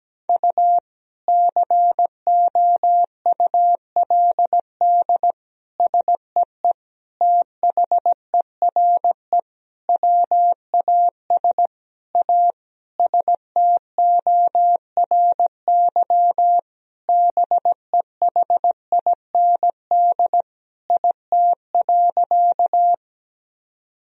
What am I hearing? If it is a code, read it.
Morse code, heard as a keyed tone, about 17 wpm, 700 Hz. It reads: U COULD SEE THERE WAS A STORY BEHIND IT.
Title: Sentences from Top 500 Words and Abbreviations 17wpm